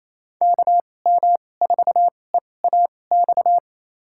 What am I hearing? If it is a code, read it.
Morse code, heard as a keyed tone, about 28 wpm, spaced as for 20 wpm, 700 Hz. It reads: KM4EAX
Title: USA Call Signs - 2x Character Spacing - 28wpm